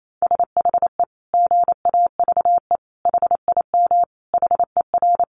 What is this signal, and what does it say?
Morse code, heard as a keyed tone, about 28 wpm, 700 Hz. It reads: SHE GAVE HIM HER
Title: Sentences from Top 1400 Words 28wpm